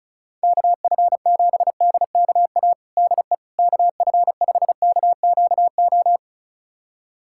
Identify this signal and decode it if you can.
Morse code, heard as a keyed tone, about 36 wpm, 700 Hz. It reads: KF7DKA DE KF5KQO
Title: USA Callsign DE Callsign - 6 Characters - 35wpm